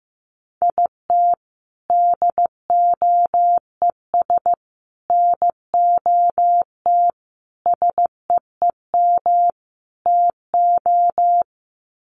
Morse code, heard as a keyed tone, about 15 wpm, 700 Hz. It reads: IT DOES NOT SEEM TO
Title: Sets of 5 Words Encore2 15wpm